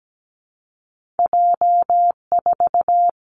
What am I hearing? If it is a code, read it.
Morse code, heard as a keyed tone, about 17 wpm, 700 Hz. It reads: J4
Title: Two Challenging Letter-Number Combinations 17wpm